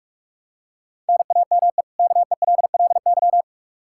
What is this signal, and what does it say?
Morse code, heard as a keyed tone, about 46 wpm, 700 Hz. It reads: NAME KELLY
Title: Practice QSO - Exchange 1 - Beginning - 45wpm